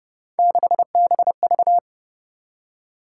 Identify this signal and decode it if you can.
Morse code, heard as a keyed tone, about 29 wpm, 700 Hz. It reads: 6BV
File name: Speed-Racing - Three Letter-Number Combinations - 25wpm